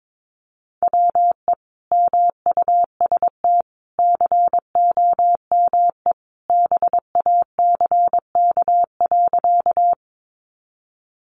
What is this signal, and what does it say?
Morse code, heard as a keyed tone, about 22 wpm, 700 Hz. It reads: WE MUST COME BACK.
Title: Sentences from Top 200 Words 22wpm